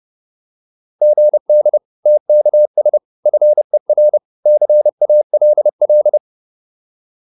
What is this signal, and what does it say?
Morse code, heard as a keyed tone, about 30 wpm, 600 Hz. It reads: GD TKS FER CALL
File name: QSO402 30